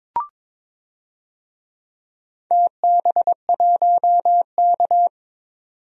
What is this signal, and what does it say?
Morse code, heard as a keyed tone, about 22 wpm, 700 Hz. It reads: TB1K
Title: DXCC-100 Call Signs - 4 Characters - 22wpm